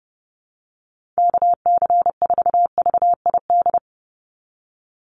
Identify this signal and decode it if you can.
Morse code, heard as a keyed tone, about 30 wpm, 700 Hz. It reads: KC4VID
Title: USA Call Signs 30wpm